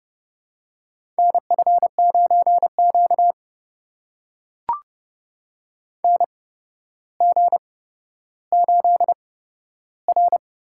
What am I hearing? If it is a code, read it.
Morse code, heard as a keyed tone, about 30 wpm, spaced as for 13 wpm, 700 Hz. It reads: NF9Q NG8R
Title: USA Call Signs - 4 Characters - 8x Character Spacing - 30wpm